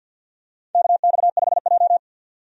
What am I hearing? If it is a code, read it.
Morse code, heard as a keyed tone, about 53 wpm, 700 Hz. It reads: KX5J
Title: USA Call Signs - 4 Characters - 50wpm